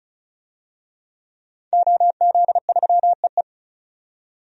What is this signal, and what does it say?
Morse code, heard as a keyed tone, about 34 wpm, 700 Hz. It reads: OZ3EE
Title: DXCC-100 Call Signs - 5 Characters - 35wpm